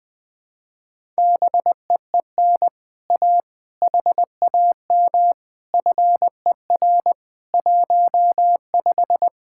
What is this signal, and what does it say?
Morse code, heard as a keyed tone, about 20 wpm, 700 Hz. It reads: BEEN A HAM FER 15